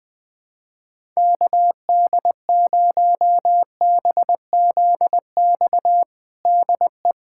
Morse code, heard as a keyed tone, about 21 wpm, 700 Hz. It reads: KD0BZX DE
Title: USA Callsign DE Callsign - 6 Characters - 20wpm